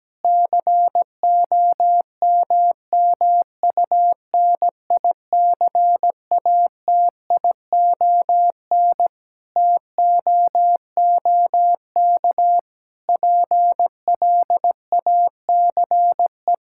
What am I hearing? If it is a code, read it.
Morse code, heard as a keyed tone, about 17 wpm, 700 Hz. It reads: COMMUNICATION TOOK PLACE
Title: Sentences from Top 1000 Words - Review Part 6 - 17wpm